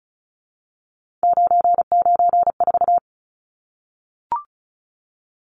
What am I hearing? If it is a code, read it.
Morse code, heard as a keyed tone, about 34 wpm, 700 Hz. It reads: 994
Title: Numbers - 3-Digit 35wpm